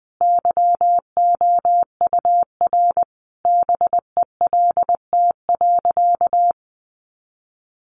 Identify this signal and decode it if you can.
Morse code, heard as a keyed tone, about 20 wpm, 700 Hz. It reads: YOUR BELT.
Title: Sentences from Top 1400 Words 20wpm